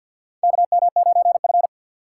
Morse code, heard as a keyed tone, about 53 wpm, 700 Hz. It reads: KM9F
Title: USA Call Signs - 4 Characters - 6x Character Spacing - 50wpm